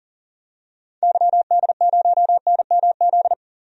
Morse code, heard as a keyed tone, about 40 wpm, 700 Hz. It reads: YD0NMZ
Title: DXCC-100 Call Signs - 6 Characters - 40wpm